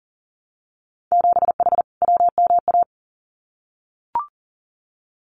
Morse code, heard as a keyed tone, about 40 wpm, 700 Hz. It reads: 7H WMA
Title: Winter Field Day Exchanges Encore 40wpm